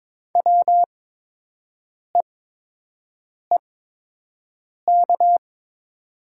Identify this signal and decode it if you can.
Morse code, heard as a keyed tone, about 23 wpm, spaced as for 6 wpm, 700 Hz. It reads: WEEK
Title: 8x Character Spacing - Top 500 Words - 22wpm